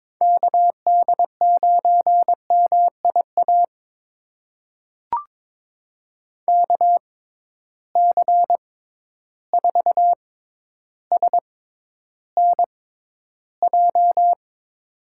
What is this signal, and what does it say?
Morse code, heard as a keyed tone, about 22 wpm, spaced as for 11 wpm, 700 Hz. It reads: KD9MIA KC4SNJ
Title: USA Call Signs - 6 Characters - 6x Character Spacing - 22wpm